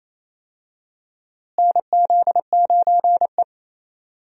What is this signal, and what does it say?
Morse code, heard as a keyed tone, about 28 wpm, 700 Hz. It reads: NZ9E
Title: USA Call Signs - 4 Characters - Repeated 3x - 28wpm